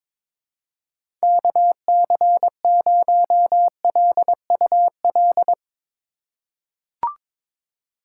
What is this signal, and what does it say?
Morse code, heard as a keyed tone, about 22 wpm, 700 Hz. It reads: KC0LUL
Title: USA Call Signs - 6 Characters - 6x Character Spacing - 22wpm